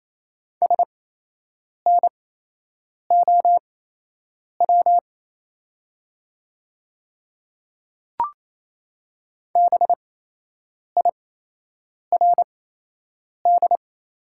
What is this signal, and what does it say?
Morse code, heard as a keyed tone, about 28 wpm, spaced as for 8 wpm, 700 Hz. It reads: SNOW BIRD
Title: 8x Character Spacing - Top 500 Words - 28wpm